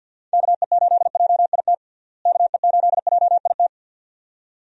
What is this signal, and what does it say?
Morse code, heard as a keyed tone, about 47 wpm, 700 Hz. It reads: KE8JIT KE8JIT
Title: USA Call Signs - 6 Characters - Repeated 2x - 50wpm